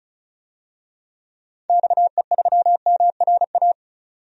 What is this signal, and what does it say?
Morse code, heard as a keyed tone, about 36 wpm, 700 Hz. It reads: XE3MRA
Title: DXCC-100 Call Signs - 6 Characters - 35wpm